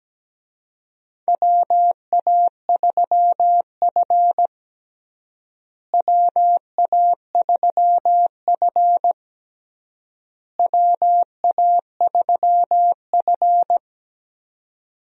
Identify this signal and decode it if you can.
Morse code, heard as a keyed tone, about 17 wpm, 700 Hz. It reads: WA3F WA3F WA3F
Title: USA Call Signs - 4 Characters - Repeated 3x - 17wpm